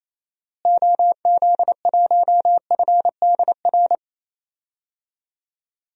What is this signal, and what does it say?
Morse code, heard as a keyed tone, about 28 wpm, 700 Hz. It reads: OZ1FDR
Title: DXCC-100 Call Signs - 6 Characters - Repeated 2x - 28wpm